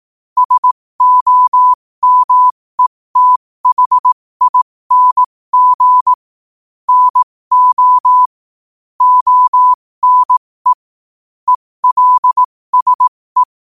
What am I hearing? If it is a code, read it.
Morse code, heard as a keyed tone, about 18 wpm, spaced as for 15 wpm, 1000 Hz. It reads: SOMETHING NO ONE ELSE
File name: qotdfiles / 15wpm